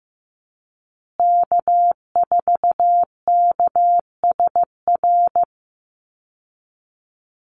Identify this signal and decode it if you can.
Morse code, heard as a keyed tone, about 15 wpm, 700 Hz. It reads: K4KSR
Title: Call Signs Repeated 2x - 15wpm